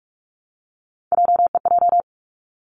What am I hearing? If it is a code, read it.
Morse code, heard as a keyed tone, about 46 wpm, 700 Hz. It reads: JEJ
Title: Three Challenging Letter-Number Combinations 45wpm